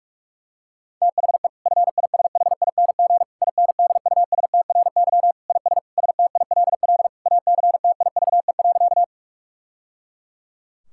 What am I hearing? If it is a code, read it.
Morse code, heard as a keyed tone, about 41 wpm, 700 Hz. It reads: THE FISHING INDUSTRY IS STILL ACTIVE.
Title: Sentences from Top 1000 Words - Review Part 6 - 45wpm